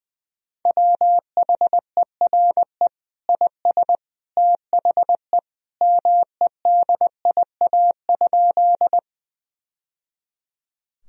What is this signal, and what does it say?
Morse code, heard as a keyed tone, about 21 wpm, 700 Hz. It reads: WHERE IS THE MEDIA?
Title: Sentences from Top 1000 Words - Review Part 4 - 20wpm